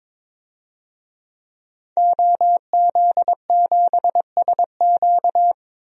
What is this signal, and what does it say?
Morse code, heard as a keyed tone, about 22 wpm, 700 Hz. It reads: OZ7SQ
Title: DXCC-100 Call Signs - 5 Characters - 22wpm